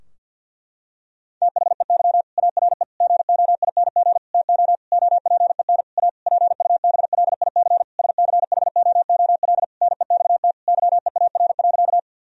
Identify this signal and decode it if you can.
Morse code, heard as a keyed tone, about 52 wpm, 700 Hz. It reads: THEY ARE GOING TO OPEN A PUBLIC SCHOOL NEXT YEAR.
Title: Sentences from Top 1000 Words - Review Part 3 - 50wpm